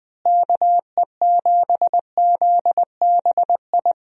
Morse code, heard as a keyed tone, about 20 wpm, 700 Hz. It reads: KE7ZBI
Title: USA Call Signs - 6 Characters - Repeated 2x - 20wpm